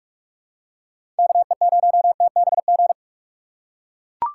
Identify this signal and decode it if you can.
Morse code, heard as a keyed tone, about 44 wpm, 700 Hz. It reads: KE0TBG
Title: USA Call Signs - 6 Characters - 6x Character Spacing - 45wpm